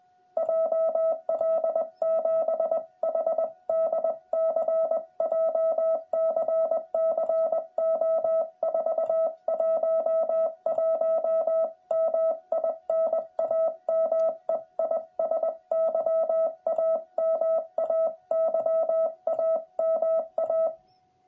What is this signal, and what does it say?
Morse code, heard as a keyed tone, about 20 wpm, 650 Hz. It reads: JL7HDC JCC O411 MI NAME IS YAMA YAMA
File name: jh8cbh / 2021cw